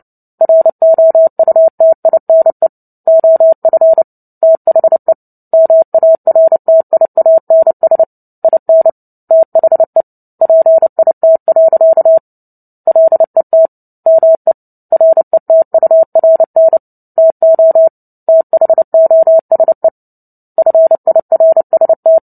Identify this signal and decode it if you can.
Morse code, heard as a keyed tone, about 30 wpm, 650 Hz. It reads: ROUTINE OF THE MARTIANS IN THE PIT. LET ME RETURN TO THOSE FIRST
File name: Book Two / 03-Chapter 3 - 29 WPM